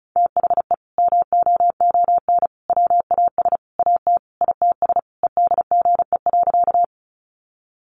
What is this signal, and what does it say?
Morse code, heard as a keyed tone, about 34 wpm, 700 Hz. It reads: THE MOON WAS AT ITS EDGE.
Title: Sentences from Top 1300 Words 35wpm